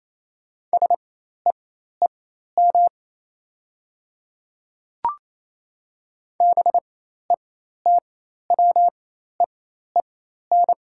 Morse code, heard as a keyed tone, about 28 wpm, spaced as for 13 wpm, 700 Hz. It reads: SEEM BETWEEN
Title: 4x Character Spacing - Top 500 Words - 28wpm